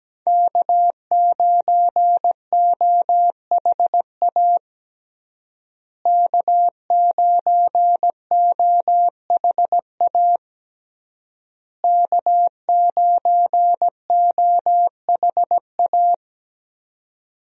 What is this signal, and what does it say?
Morse code, heard as a keyed tone, about 17 wpm, 700 Hz. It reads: K9OHA K9OHA K9OHA
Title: USA Call Signs - 5 Characters - Repeated 3x - 17wpm